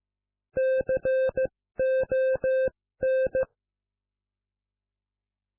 Morse code, heard as a keyed tone, about 15 wpm, spaced as for 13 wpm, 520 Hz. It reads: CON